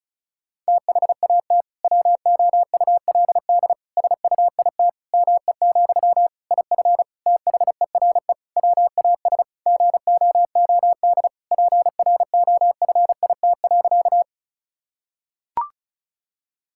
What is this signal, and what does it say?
Morse code, heard as a keyed tone, about 36 wpm, 700 Hz. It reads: THAT WOULD SUIT ME, IF THERE WAS GOOD PROFIT.
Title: Sentences from Top 1600 Words 35wpm